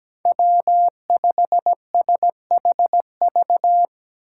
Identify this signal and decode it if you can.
Morse code, heard as a keyed tone, about 17 wpm, 700 Hz. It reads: W5SHV
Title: USA Call Signs - 5 Characters - 4x Character Spacing - 17wpm